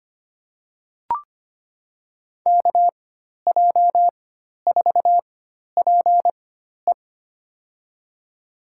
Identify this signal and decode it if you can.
Morse code, heard as a keyed tone, about 25 wpm, spaced as for 12 wpm, 700 Hz. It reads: KJ4PE
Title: USA Call Signs - 5 Characters - 4x Character Spacing - 25wpm